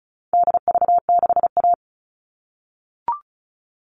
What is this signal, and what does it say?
Morse code, heard as a keyed tone, about 34 wpm, 700 Hz. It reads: DV6A
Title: All Cut Numbers - 4-Digit 35wpm